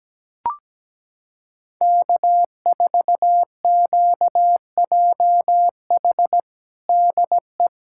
Morse code, heard as a keyed tone, about 17 wpm, 700 Hz. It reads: K4QJH DE
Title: Practice QSO - Exchange 4 - Ending - 17wpm